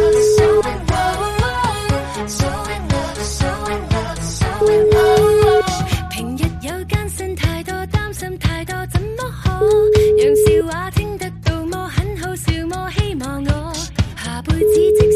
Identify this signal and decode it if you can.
Ringback tone (Europe, 425 Hz), with noise.